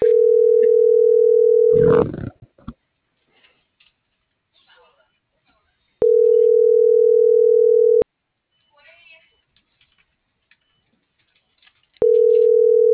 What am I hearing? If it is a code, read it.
Ringback tone (North America), 440 + 480 Hz.
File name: CENTRAL MORENA / out dulce.morena.2024